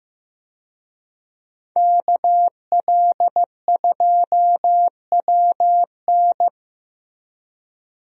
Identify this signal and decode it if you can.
Morse code, heard as a keyed tone, about 16 wpm, 700 Hz. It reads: KL2WN
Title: USA Call Signs - 5 Characters - Repeated 3x - 15wpm